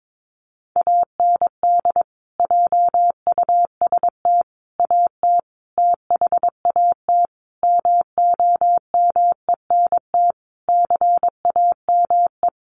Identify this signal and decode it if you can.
Morse code, heard as a keyed tone, about 22 wpm, 700 Hz. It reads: AND JUST AT THAT MOMENT CAME
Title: Sentences from Top 1500 Words 22wpm